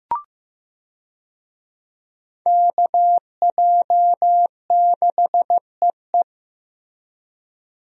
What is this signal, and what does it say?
Morse code, heard as a keyed tone, about 15 wpm, 700 Hz. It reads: KJ6EE
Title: USA Call Signs - 5 Characters - 15wpm